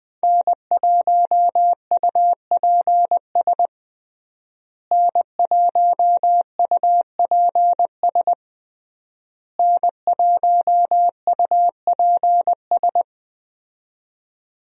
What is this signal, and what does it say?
Morse code, heard as a keyed tone, about 21 wpm, 700 Hz. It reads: N1UPS N1UPS N1UPS
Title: USA Call Signs - 5 Characters - Repeated 3x - 20wpm